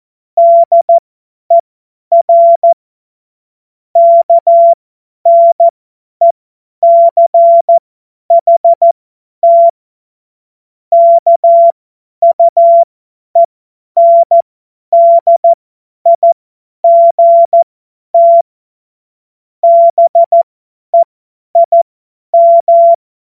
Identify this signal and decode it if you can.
Morse code, heard as a keyed tone, about 14 wpm, spaced as for 10 wpm, 680 Hz. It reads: DER KNECHT KUENDIGT BEIM
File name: Morsekurs / Witze / Witz 04. April 2023 50 F70